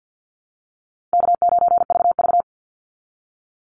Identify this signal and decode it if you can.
Morse code, heard as a keyed tone, about 50 wpm, 700 Hz. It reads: K9VV
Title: Call Signs Repeated 2x - 50wpm